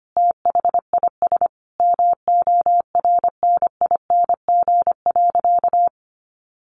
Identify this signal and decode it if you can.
Morse code, heard as a keyed tone, about 25 wpm, 700 Hz. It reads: THIS MORNING.
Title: Sentences from Top 900 Words 25wpm